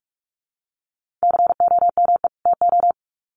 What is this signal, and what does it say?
Morse code, heard as a keyed tone, about 44 wpm, 700 Hz. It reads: COME TO
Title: Sets of 2 Words 45wpm